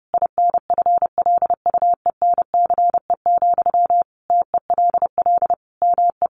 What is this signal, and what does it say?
Morse code, heard as a keyed tone, about 30 wpm, 700 Hz. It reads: INFLUENCE, TELL ME
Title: Sentences from Top 1100 Words 30wpm